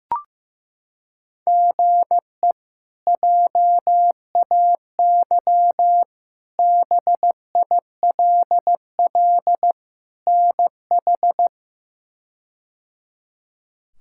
Morse code, heard as a keyed tone, about 15 wpm, 700 Hz. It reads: GE JAY BILL NH
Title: K1USN SST QSO Practice - 15wpm